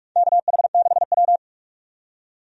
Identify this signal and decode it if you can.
Morse code, heard as a keyed tone, about 44 wpm, 700 Hz. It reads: KH6W
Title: DXCC-100 Call Signs - 4 Characters - 45wpm